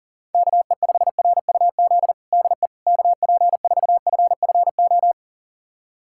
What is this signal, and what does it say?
Morse code, heard as a keyed tone, about 40 wpm, 700 Hz. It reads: KE5RUZ DE KP4FFO
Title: USA Callsign DE Callsign - 6 Characters - 40wpm